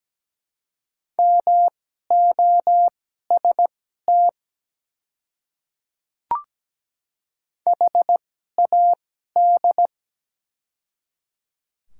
Morse code, heard as a keyed tone, about 17 wpm, spaced as for 12 wpm, 700 Hz. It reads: MOST HAD
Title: 2x Character Spacing - Top 100 Words - 17wpm